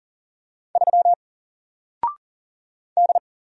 Morse code, heard as a keyed tone, about 40 wpm, 700 Hz. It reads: 3 D
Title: Single Letter-Number Alt 40wpm